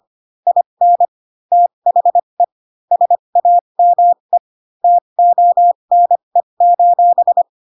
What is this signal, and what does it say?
Morse code, heard as a keyed tone, about 25 wpm, spaced as for 22 wpm, 700 Hz. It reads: IN THE SAME TONE:
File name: Pinocchio / Pinocchio ch4 18 25 ik1hge